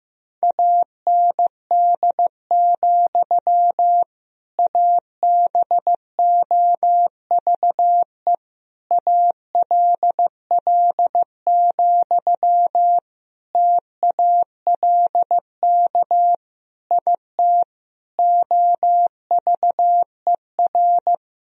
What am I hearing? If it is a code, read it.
Morse code, heard as a keyed tone, about 16 wpm, 700 Hz. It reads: AND, ABOVE ALL, TALK IT OVER
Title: Sentences from Top 1000 Words - Review Part 2 - 15wpm